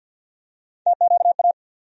Morse code, heard as a keyed tone, about 51 wpm, 700 Hz. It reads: TQA
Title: USA Call Sign Postfixes - 3 Letters 50wpm